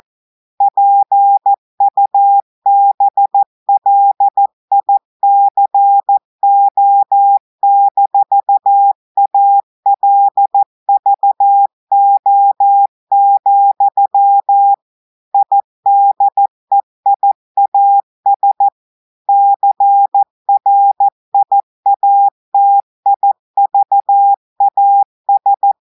Morse code, heard as a keyed tone, about 14 wpm, 800 Hz. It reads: PUBLICO-ALVO, IDEIAS CRIATIVAS
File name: audios / portugais / portugais texte en clair 2 vitesse 14 mots minutes